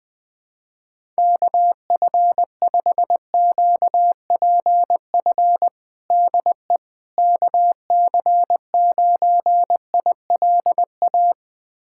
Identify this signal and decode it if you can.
Morse code, heard as a keyed tone, about 20 wpm, 700 Hz. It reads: KF5QPF DE KC9ILA
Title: USA Callsign DE Callsign 20wpm